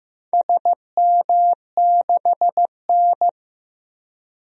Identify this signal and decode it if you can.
Morse code, heard as a keyed tone, about 15 wpm, 700 Hz. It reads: SM6N
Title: DXCC-100 Call Signs - 4 Characters - Repeated 2x - 15wpm